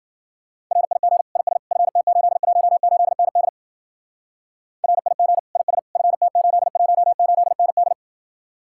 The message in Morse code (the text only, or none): RIG IS FT818ND RIG IS FT818ND